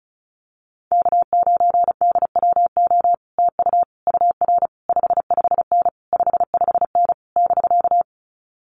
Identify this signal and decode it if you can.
Morse code, heard as a keyed tone, about 36 wpm, 700 Hz. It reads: K9DWO TU UR 55N 55N <BK>
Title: Common QSO Phrases Encore 35wpm